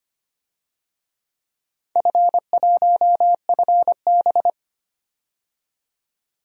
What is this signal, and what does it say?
Morse code, heard as a keyed tone, about 25 wpm, 700 Hz. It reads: F1FB
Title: DXCC-100 Call Signs - 4 Characters - 25wpm